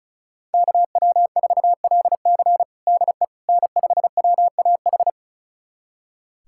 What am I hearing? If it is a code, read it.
Morse code, heard as a keyed tone, about 34 wpm, 700 Hz. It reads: KW4LC DE N5WAH
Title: USA Callsign DE Callsign - 5 Characters - 35wpm